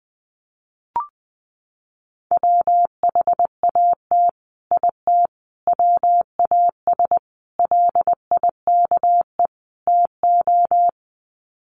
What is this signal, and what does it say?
Morse code, heard as a keyed tone, about 19 wpm, 700 Hz. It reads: WHAT IT WAS LIKE TO
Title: Sets of 5 Words Encore 20wpm